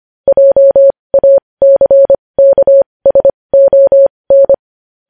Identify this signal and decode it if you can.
Morse code, heard as a keyed tone, about 25 wpm, spaced as for 20 wpm, 550 Hz. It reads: JACKSON
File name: podcast / News headlines-1x-25x20